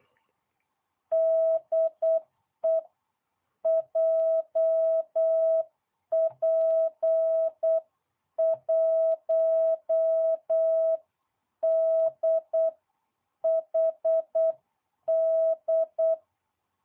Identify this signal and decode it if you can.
Morse code, heard as a keyed tone, about 8 wpm, 660 Hz. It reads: DE JP1DHD